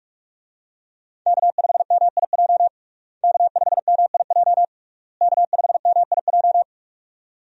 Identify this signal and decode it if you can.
Morse code, heard as a keyed tone, about 42 wpm, 700 Hz. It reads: K5MIJ K5MIJ K5MIJ
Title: USA Call Signs - 5 Characters - Repeated 3x - 45wpm